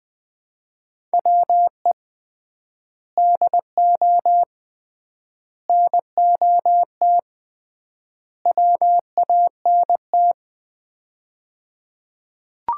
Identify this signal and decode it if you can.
Morse code, heard as a keyed tone, about 21 wpm, 700 Hz. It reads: WE DO NOT WANT
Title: 3x Word-Spacing - Sets of 4 Words - 20wpm